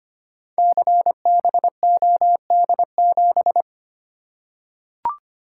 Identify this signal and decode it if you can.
Morse code, heard as a keyed tone, about 25 wpm, 700 Hz. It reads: CBOD7
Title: Code Groups - 5 Letter:Numbers - 25wpm